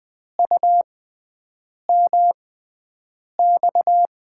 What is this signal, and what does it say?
Morse code, heard as a keyed tone, about 20 wpm, spaced as for 7 wpm, 700 Hz. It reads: UMX
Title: USA Call Signs - 5 Characters - 6x Character Spacing - 20wpm